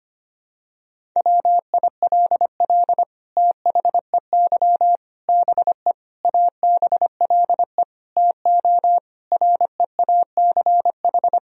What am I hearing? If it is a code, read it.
Morse code, heard as a keyed tone, about 25 wpm, 700 Hz. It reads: WILL THEY BE ABLE TO REACH
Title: Sentences from Top 1000 Words - Review Part 2 - 25wpm